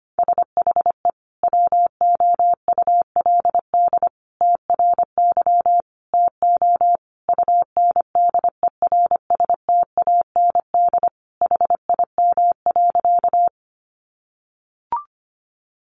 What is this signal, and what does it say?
Morse code, heard as a keyed tone, about 25 wpm, 700 Hz. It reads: SHE WOULD TRY TO UNDERSTAND HIM.
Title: Sentences from Top 600 Words 25wpm